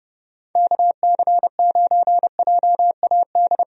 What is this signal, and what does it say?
Morse code, heard as a keyed tone, about 30 wpm, 700 Hz. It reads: KC9JAD
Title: USA Call Signs - 6 Characters - 30wpm